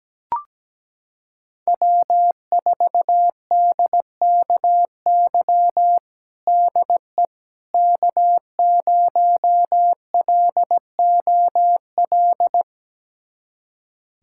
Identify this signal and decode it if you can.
Morse code, heard as a keyed tone, about 17 wpm, 700 Hz. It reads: W4DKY DE K0LOL
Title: USA Callsign DE Callsign - 5 Characters - 17wpm